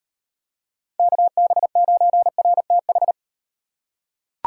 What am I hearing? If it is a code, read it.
Morse code, heard as a keyed tone, about 38 wpm, 700 Hz. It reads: KB9RTH